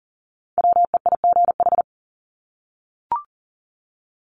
Morse code, heard as a keyed tone, about 40 wpm, 700 Hz. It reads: WEIGH
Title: Top 1751-2000 Words 40wpm